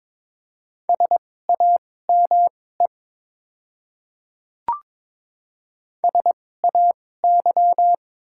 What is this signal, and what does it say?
Morse code, heard as a keyed tone, about 21 wpm, spaced as for 16 wpm, 700 Hz. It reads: SAME SAY
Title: 2x Character Spacing - Top 200 Words - 22wpm